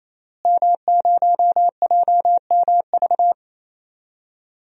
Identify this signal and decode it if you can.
Morse code, heard as a keyed tone, about 28 wpm, 700 Hz. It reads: M0JMV
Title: DXCC-100 Call Signs - 5 Characters - Repeated 3x - 28wpm